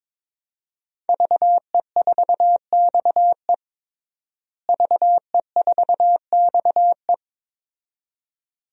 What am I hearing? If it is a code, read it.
Morse code, heard as a keyed tone, about 22 wpm, 700 Hz. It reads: VE4XE VE4XE
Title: DXCC-100 Call Signs - 5 Characters - Repeated 2x - 22wpm